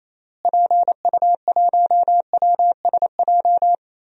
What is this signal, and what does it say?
Morse code, heard as a keyed tone, about 28 wpm, 700 Hz. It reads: PU1WSJ
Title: DXCC-100 Call Signs - 6 Characters - Repeated 3x - 28wpm